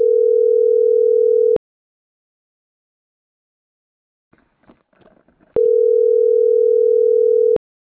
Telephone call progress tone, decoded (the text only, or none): ringback tone